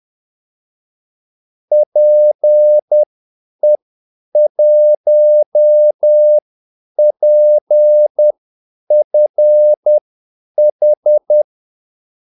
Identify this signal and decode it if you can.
Morse code, heard as a keyed tone, about 10 wpm, spaced as for 8 wpm, 600 Hz. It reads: PE1PFH
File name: Callsigns-10-8wpm